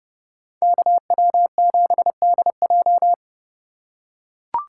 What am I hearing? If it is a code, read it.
Morse code, heard as a keyed tone, about 30 wpm, 700 Hz. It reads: KW7DJ